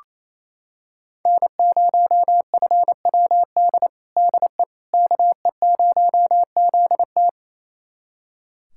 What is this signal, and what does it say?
Morse code, heard as a keyed tone, about 28 wpm, 700 Hz. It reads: N0FWD DE KE0ZT
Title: USA Callsign DE Callsign - 5 Characters - 28wpm